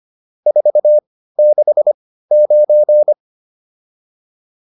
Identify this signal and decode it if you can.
Morse code, heard as a keyed tone, about 25 wpm, spaced as for 15 wpm, 600 Hz. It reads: 469